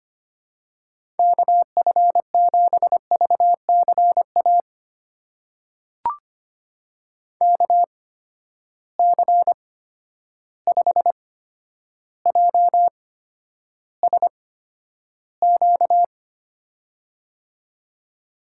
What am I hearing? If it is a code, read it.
Morse code, heard as a keyed tone, about 25 wpm, spaced as for 11 wpm, 700 Hz. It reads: KF7VCA KC5JSQ
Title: USA Call Signs - 8x Character Spacing - 25wpm